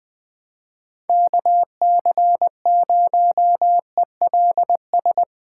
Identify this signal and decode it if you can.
Morse code, heard as a keyed tone, about 19 wpm, 700 Hz. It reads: KC0ELS
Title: Speed Racing - USA 2x3 Call Signs 17wpm